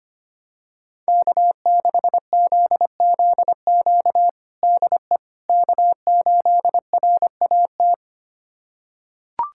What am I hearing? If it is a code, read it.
Morse code, heard as a keyed tone, about 25 wpm, 700 Hz. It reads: K6ZZQ DE K8RAT
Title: USA Callsign DE Callsign 25wpm